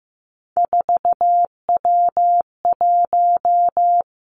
Morse code, heard as a keyed tone, about 15 wpm, 700 Hz. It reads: 4W1
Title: Three Challenging Letter-Number Combinations 15wpm